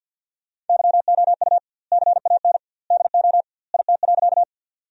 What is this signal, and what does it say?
Morse code, heard as a keyed tone, about 47 wpm, 700 Hz. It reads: YOU CAN DO IT.
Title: 2x Word-Spacing - Sentences from Top 100 Words - 50wpm